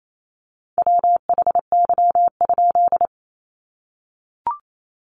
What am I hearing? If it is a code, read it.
Morse code, heard as a keyed tone, about 28 wpm, 700 Hz. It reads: WHY?
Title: A Sound Of Thunder 28wpm